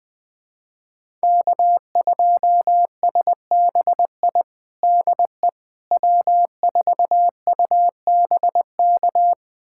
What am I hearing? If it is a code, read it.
Morse code, heard as a keyed tone, about 21 wpm, 700 Hz. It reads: K2SBI DE W4UBK
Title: USA Callsign DE Callsign - 5 Characters - 20wpm